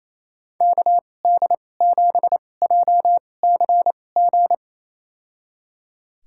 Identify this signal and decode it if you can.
Morse code, heard as a keyed tone, about 28 wpm, spaced as for 20 wpm, 700 Hz. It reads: KD7JCG
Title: USA Call Signs - 2x Character Spacing - 28wpm